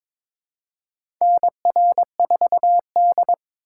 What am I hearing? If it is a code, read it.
Morse code, heard as a keyed tone, about 22 wpm, 700 Hz. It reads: NR4D
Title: USA Call Signs - 4 Characters - 4x Character Spacing - 22wpm